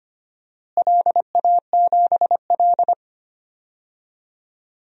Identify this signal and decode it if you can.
Morse code, heard as a keyed tone, about 25 wpm, 700 Hz. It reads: LA7L
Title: DXCC-100 Call Signs - 4 Characters - Repeated 3x - 25wpm